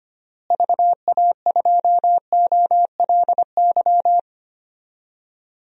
Morse code, heard as a keyed tone, about 25 wpm, 700 Hz. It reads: VA2OLY
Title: DXCC-100 Call Signs - 6 Characters - Repeated 2x - 25wpm